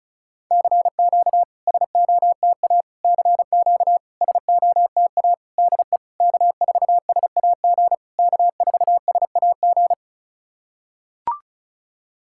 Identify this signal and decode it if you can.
Morse code, heard as a keyed tone, about 36 wpm, 700 Hz. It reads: CQ SOTA CQ SOTA DE K4SAG K4SAG
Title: SOTA QSO Practice 35wpm